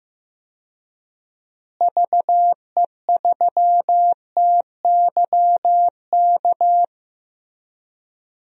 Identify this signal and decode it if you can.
Morse code, heard as a keyed tone, about 15 wpm, 700 Hz. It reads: VE3TYK
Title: DXCC-100 Call Signs - 6 Characters - Repeated 3x - 15wpm